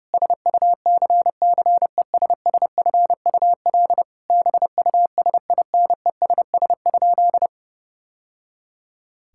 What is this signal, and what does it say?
Morse code, heard as a keyed tone, about 32 wpm, 700 Hz. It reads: SUCCESSFUL BUSINESS?
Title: Sentences from Top 1000 Words - Review Part 6 - 30wpm